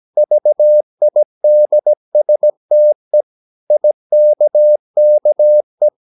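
Morse code, heard as a keyed tone, about 17 wpm, 600 Hz. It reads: VIDSTE IKKE
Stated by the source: Slow (QRS) Morse practice at 17 wpm, from CW med Gnister.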